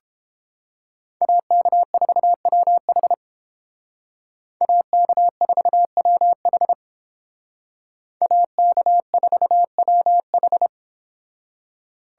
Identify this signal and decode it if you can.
Morse code, heard as a keyed tone, about 29 wpm, 700 Hz. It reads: AK4WH AK4WH AK4WH